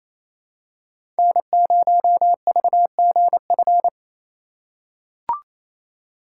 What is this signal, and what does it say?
Morse code, heard as a keyed tone, about 28 wpm, 700 Hz. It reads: N0VGF
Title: USA Call Signs - 5 Characters - 2x Character Spacing - 28wpm